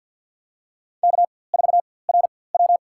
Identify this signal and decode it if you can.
Morse code, heard as a keyed tone, about 53 wpm, spaced as for 24 wpm, 700 Hz. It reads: K4RW
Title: USA Call Signs - 4x Character Spacing - 50wpm